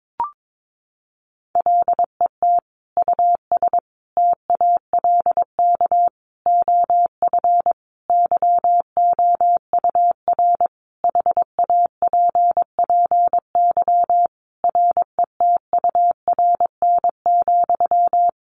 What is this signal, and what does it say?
Morse code, heard as a keyed tone, about 22 wpm, 700 Hz. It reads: LET US TALK OF YOUR HAPPY RETURN,
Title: Sentences from Top 1000 Words 22wpm